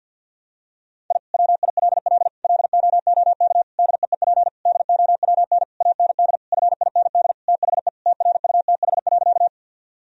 Morse code, heard as a keyed tone, about 50 wpm, 700 Hz. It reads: I WILL LOOK DEEP DOWN AND FIND THE TRUTH.
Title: Sentences from Top 1000 Words - Review Part 3 - 50wpm